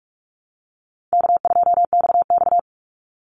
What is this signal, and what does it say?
Morse code, heard as a keyed tone, about 44 wpm, 700 Hz. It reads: K2XX
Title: Easy Call Signs Repeated 3x - 45wpm